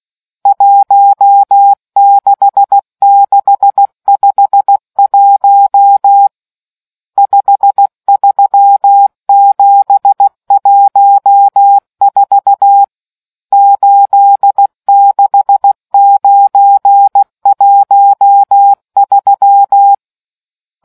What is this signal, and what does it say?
Morse code, heard as a keyed tone, about 16 wpm, 800 Hz. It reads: 16651 53714 86913